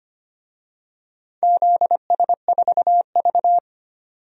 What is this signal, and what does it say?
Morse code, heard as a keyed tone, about 25 wpm, 700 Hz. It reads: ZS4V
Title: DXCC-100 Call Signs - 4 Characters - 25wpm